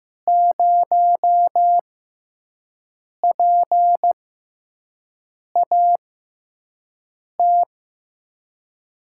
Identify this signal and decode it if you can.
Morse code, heard as a keyed tone, about 15 wpm, spaced as for 5 wpm, 700 Hz. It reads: 0PAT
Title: USA Call Signs - 6 Characters - 6x Character Spacing - 15wpm